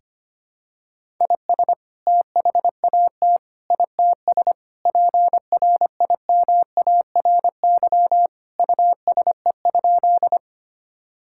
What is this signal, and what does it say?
Morse code, heard as a keyed tone, about 25 wpm, 700 Hz. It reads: IS THAT ITS PRIMARY USE?
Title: Sentences from Top 1000 Words - Review Part 5 - 25wpm